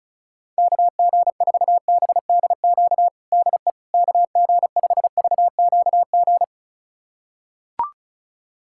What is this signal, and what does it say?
Morse code, heard as a keyed tone, about 34 wpm, 700 Hz. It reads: KG4BDQ DE KG5VQG
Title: USA Callsign DE Callsign 35wpm